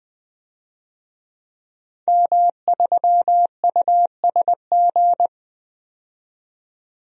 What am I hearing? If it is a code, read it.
Morse code, heard as a keyed tone, about 20 wpm, 700 Hz. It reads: M3USG
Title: DXCC-100 Call Signs - 5 Characters - 20wpm